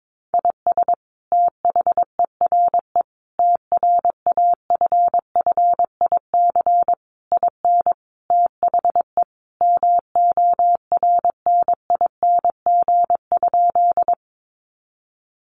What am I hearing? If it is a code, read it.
Morse code, heard as a keyed tone, about 22 wpm, 700 Hz. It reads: IS THERE TRAFFIC IN THE MORNING?
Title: Sentences from Top 1300 Words 22wpm